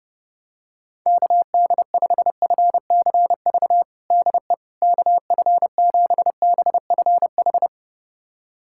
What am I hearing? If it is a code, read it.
Morse code, heard as a keyed tone, about 30 wpm, 700 Hz. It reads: KD5FCV DE KF7BFH
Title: USA Callsign DE Callsign 30wpm